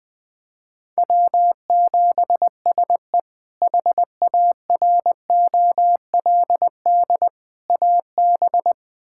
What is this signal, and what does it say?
Morse code, heard as a keyed tone, about 21 wpm, 700 Hz. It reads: W7SE HAROLD AB
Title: K1USN SST QSO Practice - 20wpm